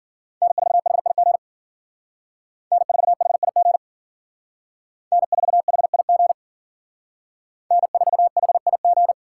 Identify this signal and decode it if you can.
Morse code, heard as a keyed tone, about 51 wpm, 690 Hz. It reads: N4HIG N4HIG N4HIG N4HIG